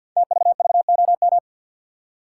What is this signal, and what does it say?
Morse code, heard as a keyed tone, about 50 wpm, 700 Hz. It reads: TVVOM
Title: Code Groups - 5 Letters - 50wpm